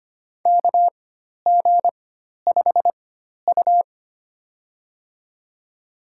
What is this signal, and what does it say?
Morse code, heard as a keyed tone, about 25 wpm, spaced as for 12 wpm, 700 Hz. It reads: KG5U
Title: USA Call Signs - 4 Characters - 4x Character Spacing - 25wpm